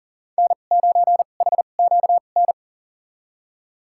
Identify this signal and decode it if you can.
Morse code, heard as a keyed tone, about 40 wpm, spaced as for 29 wpm, 700 Hz. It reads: N9HQN
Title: USA Call Signs - 5 Characters - 2x Character Spacing - 40wpm